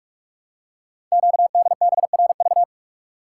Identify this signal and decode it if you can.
Morse code, heard as a keyed tone, about 44 wpm, 700 Hz. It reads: QDBRV
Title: Code Groups - 5 Letters - 45wpm